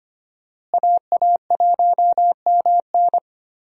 Morse code, heard as a keyed tone, about 25 wpm, 700 Hz. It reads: AA1MN
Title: USA Call Signs - 5 Characters - Repeated 3x - 25wpm